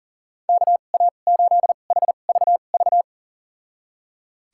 Morse code, heard as a keyed tone, about 37 wpm, spaced as for 28 wpm, 700 Hz. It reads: KA8HVV
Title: USA Call Signs - 2x Character Spacing - 40wpm